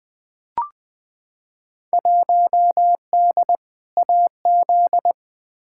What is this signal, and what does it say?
Morse code, heard as a keyed tone, about 20 wpm, 700 Hz. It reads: 1D AZ
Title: Field Day Exchanges - 20wpm